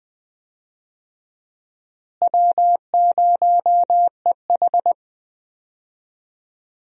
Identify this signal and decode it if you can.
Morse code, heard as a keyed tone, about 20 wpm, 700 Hz. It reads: W0EH
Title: USA Call Signs - 4 Characters - Repeated 3x - 20wpm